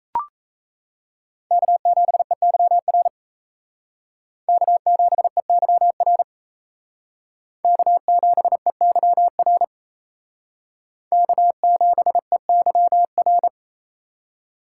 Morse code, heard as a keyed tone, about 35 wpm, 700 Hz. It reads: K7EYR K7EYR K7EYR K7EYR